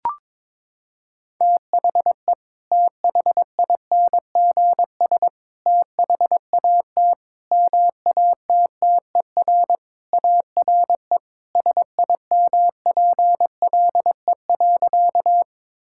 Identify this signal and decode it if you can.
Morse code, heard as a keyed tone, about 22 wpm, 700 Hz. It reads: THE THINGS THAT MATTER ARE SIMPLE.
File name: practice / 22wpm / Sentences with 3-Word Phrases - 22wpm